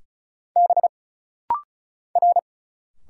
Morse code, heard as a keyed tone, about 34 wpm, 700 Hz. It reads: B R
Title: Single Letter-Number - Rapid-Fire 35wpm